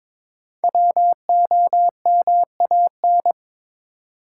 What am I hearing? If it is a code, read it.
Morse code, heard as a keyed tone, about 22 wpm, 700 Hz. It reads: WOMAN
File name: 5 Letter Words - First 3 Letters Given - 22wpm